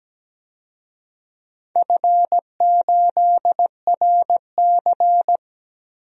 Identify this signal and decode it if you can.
Morse code, heard as a keyed tone, about 17 wpm, 700 Hz. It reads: F8RC
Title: DXCC-100 Call Signs - 4 Characters - 17wpm